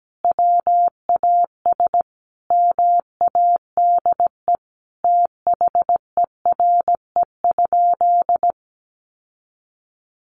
Morse code, heard as a keyed tone, about 17 wpm, 700 Hz. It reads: WAS MADE THERE?
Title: Sentences from Top 1400 Words 17wpm